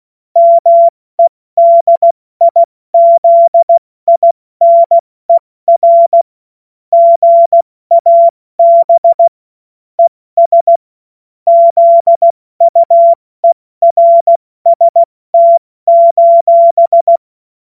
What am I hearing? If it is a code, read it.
Morse code, heard as a keyed tone, about 16 wpm, 680 Hz. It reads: MEDIZINER GAB ES ZUERST:
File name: Morsekurs / Witze / Witz 21.November 2023 70 F80